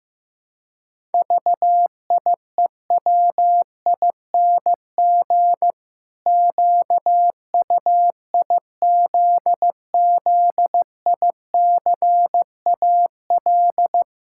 Morse code, heard as a keyed tone, about 15 wpm, 700 Hz. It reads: VIEWING QUIZZICAL
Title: Pangram Challenge v2 - 15wpm